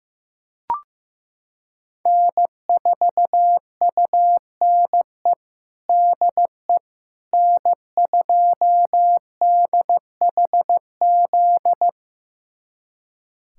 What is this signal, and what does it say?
Morse code, heard as a keyed tone, about 15 wpm, 700 Hz. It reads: N4UNE DE N2DHZ
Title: USA Callsign DE Callsign - 5 Characters - 15wpm